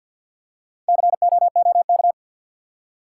Morse code, heard as a keyed tone, about 48 wpm, 700 Hz. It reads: COOK
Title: Four Letter Words - 50wpm